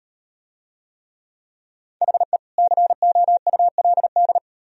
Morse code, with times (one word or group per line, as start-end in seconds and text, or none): 2.0-2.4 HE
2.6-4.4 COULD